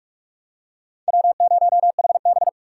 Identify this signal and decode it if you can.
Morse code, heard as a keyed tone, about 46 wpm, 700 Hz. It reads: W0HB
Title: USA Call Signs - 4 Characters - 2x Character Spacing - 45wpm